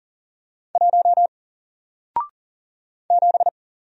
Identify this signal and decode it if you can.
Morse code, heard as a keyed tone, about 40 wpm, 700 Hz. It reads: 1 7
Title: Single Letter-Number Alt 40wpm